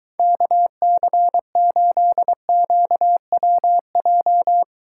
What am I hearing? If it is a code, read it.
Morse code, heard as a keyed tone, about 23 wpm, 700 Hz. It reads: KC8QWJ